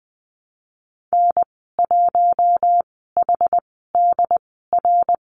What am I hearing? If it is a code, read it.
Morse code, heard as a keyed tone, about 20 wpm, spaced as for 14 wpm, 700 Hz. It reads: N1HDR
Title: 082 Lesson 14g - Number 2 - Call Signs 20wpm